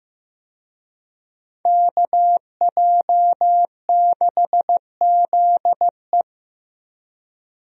Morse code, heard as a keyed tone, about 15 wpm, 700 Hz. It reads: KJ6ZE
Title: USA Call Signs - 5 Characters - Repeated 3x - 15wpm